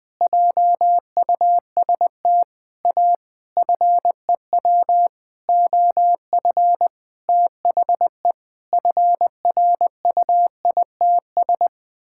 Morse code, heard as a keyed tone, about 20 wpm, 700 Hz. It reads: JUST A FEW OF THE FRUITS
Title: Sentences with 6-Word Phrases - Spoken Beginning - 20wpm